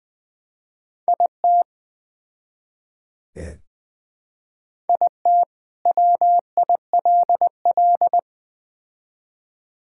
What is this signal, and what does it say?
Morse code, heard as a keyed tone, about 21 wpm, 700 Hz. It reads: IT IT WILL
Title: Build A Phrase - 3-Word Phrases - 20wpm